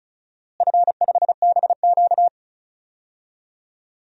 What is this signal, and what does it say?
Morse code, heard as a keyed tone, about 36 wpm, 700 Hz. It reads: F5BQ
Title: DXCC-100 Call Signs - 4 Characters - Repeated 2x - 35wpm